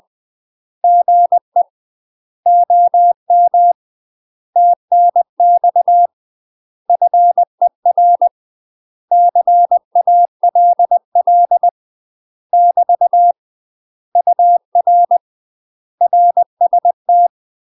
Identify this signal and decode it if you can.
Morse code, heard as a keyed tone, about 20 wpm, 700 Hz. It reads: GE OM TNX FER CALL <BT> UR RST
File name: UK MASTER QSOs TRAINING / 20WPM / UK MASTER QSOs 0037